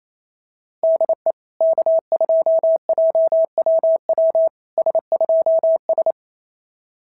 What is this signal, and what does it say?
Morse code, heard as a keyed tone, about 28 wpm, 650 Hz. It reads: DE K2JWW S2S
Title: SOTA QSO Practice 28wpm